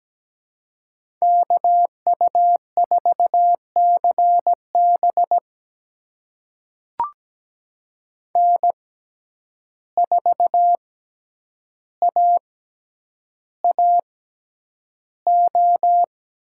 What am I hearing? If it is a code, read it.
Morse code, heard as a keyed tone, about 17 wpm, spaced as for 9 wpm, 700 Hz. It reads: KU4CB N4AAO
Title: USA Call Signs - 5 Characters - 6x Character Spacing - 17wpm